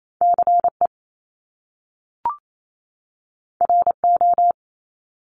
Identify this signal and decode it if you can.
Morse code, heard as a keyed tone, about 28 wpm, 700 Hz. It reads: CE RO
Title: Top Two Letter Combinations in Words 28wpm